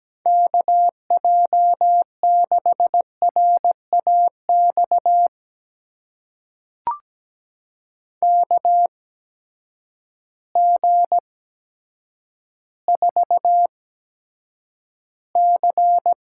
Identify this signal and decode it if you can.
Morse code, heard as a keyed tone, about 17 wpm, spaced as for 8 wpm, 700 Hz. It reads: KJ6RAX KG4C
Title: USA Call Signs - 6 Characters - 8x Character Spacing - 17wpm